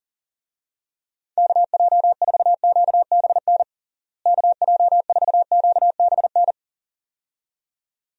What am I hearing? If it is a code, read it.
Morse code, heard as a keyed tone, about 40 wpm, 700 Hz. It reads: KJ4QBN KJ4QBN
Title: USA Call Signs - 6 Characters - Repeated 2x - 40wpm